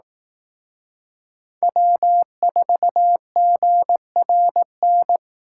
Morse code, heard as a keyed tone, about 18 wpm, 700 Hz. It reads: W4GRN
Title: Speed Racing - USA Call Signs 15wpm